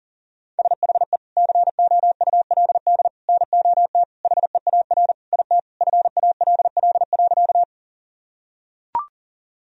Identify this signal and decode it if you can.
Morse code, heard as a keyed tone, about 37 wpm, 700 Hz. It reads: SHE COULD NOT HEAR IT FALL.
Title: Sentences from Top 1000 Words - Review Part 2 - 40wpm